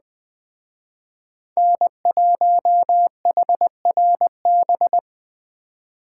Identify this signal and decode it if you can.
Morse code, heard as a keyed tone, about 20 wpm, 700 Hz. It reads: N1HRB